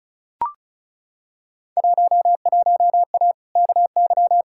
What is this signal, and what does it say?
Morse code, heard as a keyed tone, about 36 wpm, 700 Hz. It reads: 11A KY
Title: Field Day Exchanges Encore - 35wpm